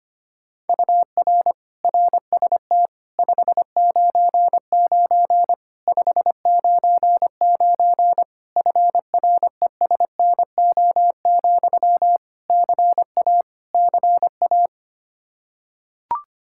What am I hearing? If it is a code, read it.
Morse code, heard as a keyed tone, about 25 wpm, 700 Hz. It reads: UR RST 599 599 FRESNO, CA CA
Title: Practice QSO - Exchange 1 - Beginning - 25wpm